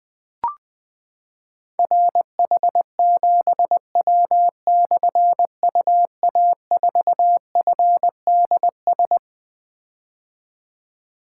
Morse code, heard as a keyed tone, about 20 wpm, 700 Hz. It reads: RH7W/UA4FDS
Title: Difficult Call Signs 20wpm